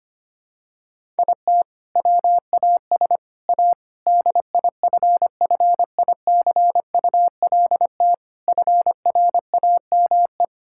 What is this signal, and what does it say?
Morse code, heard as a keyed tone, about 25 wpm, 700 Hz. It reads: IT WAS A DIFFICULT FRAME
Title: Sentences from Top 1000 Words - Review Part 6 - 25wpm